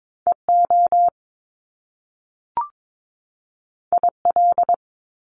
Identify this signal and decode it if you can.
Morse code, heard as a keyed tone, about 22 wpm, 700 Hz. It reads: EO IL
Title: Top Two Letter Combinations in Words 22wpm